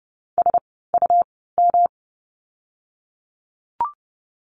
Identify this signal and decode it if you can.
Morse code, heard as a keyed tone, about 30 wpm, spaced as for 17 wpm, 700 Hz. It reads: SUM
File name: course / 30:10wpm / 108 Lesson 18f - Letter m - Words 30wpm